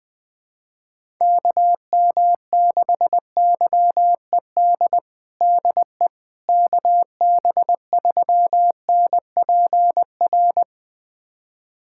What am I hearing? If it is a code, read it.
Morse code, heard as a keyed tone, about 20 wpm, 700 Hz. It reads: KM6YED DE KB3NPR
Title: USA Callsign DE Callsign - 6 Characters - 20wpm